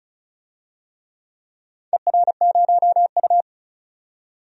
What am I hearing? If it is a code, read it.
Morse code, heard as a keyed tone, about 34 wpm, 700 Hz. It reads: ER0U
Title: DXCC-100 Call Signs - 4 Characters - 35wpm